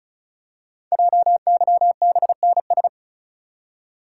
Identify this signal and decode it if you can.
Morse code, heard as a keyed tone, about 36 wpm, 700 Hz. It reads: JYBNS
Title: Code Groups - 5 Letters - 35wpm